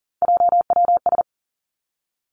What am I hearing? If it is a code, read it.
Morse code, heard as a keyed tone, about 38 wpm, 700 Hz. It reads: JWS
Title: Three Challenging Letter-Number Combinations 40wpm